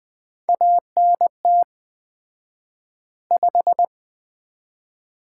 Morse code, heard as a keyed tone, about 20 wpm, 700 Hz. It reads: ANT 5
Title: Common QSO Phrases - 4x Word-Spacing - 20wpm